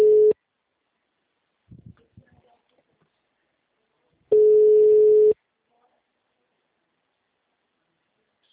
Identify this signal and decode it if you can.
Ringback tone (Europe, 425 Hz).